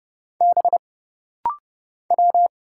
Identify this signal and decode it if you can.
Morse code, heard as a keyed tone, about 30 wpm, 700 Hz. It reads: B W
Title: Single Letters - Rapid-Fire 30wpm